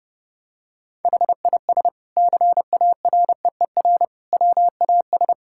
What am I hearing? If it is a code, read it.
Morse code, heard as a keyed tone, about 30 wpm, 700 Hz. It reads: HIS CAREER WAS
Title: Sentences from Top 1000 Words - Review Part 6 - 30wpm